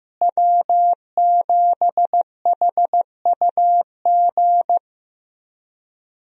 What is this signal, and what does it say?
Morse code, heard as a keyed tone, about 16 wpm, 700 Hz. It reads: W7HUG
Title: USA Call Signs - 5 Characters - Repeated 2x - 15wpm